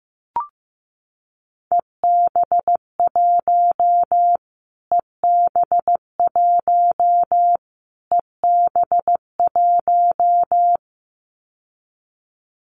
Morse code, heard as a keyed tone, about 15 wpm, 700 Hz. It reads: EB1 EB1 EB1
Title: Callsign Prefixes for Top 25 DXCC - Repeated 3x - 15wpm